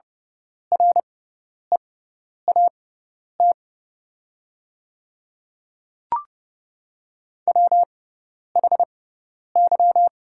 Morse code, heard as a keyed tone, about 30 wpm, spaced as for 10 wpm, 690 Hz. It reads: REAT WHY
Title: 6x Character Spacing - Top 200 Words - 30wpm